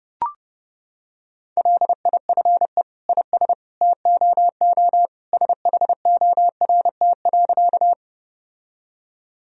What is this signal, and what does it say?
Morse code, heard as a keyed tone, about 30 wpm, 700 Hz. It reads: LIFE IS TOO SHORT.
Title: Sentences from Top 1000 Words - Review Part 2 - 30wpm